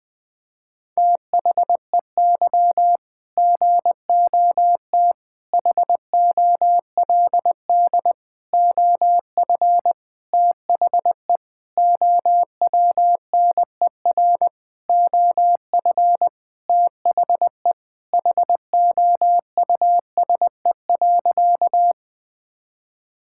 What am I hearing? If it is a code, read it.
Morse code, heard as a keyed tone, about 20 wpm, 700 Hz. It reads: THEY GOT HOLD OF THE OWNER OF THE HOUSE.
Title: Sentences from Top 1600 Words 20wpm